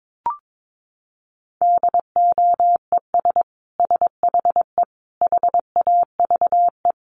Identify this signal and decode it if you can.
Morse code, heard as a keyed tone, about 22 wpm, 700 Hz. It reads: DOES SHE HAVE
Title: Sentences from Top 1400 Words 22wpm